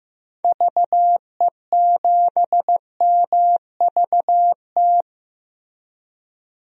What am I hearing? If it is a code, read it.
Morse code, heard as a keyed tone, about 15 wpm, 700 Hz. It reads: VE7MVT
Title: DXCC-100 Call Signs - 6 Characters - Repeated 2x - 15wpm